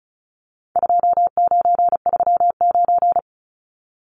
Morse code, heard as a keyed tone, about 34 wpm, 700 Hz. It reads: 2939
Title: Numbers - 4-Digit 35wpm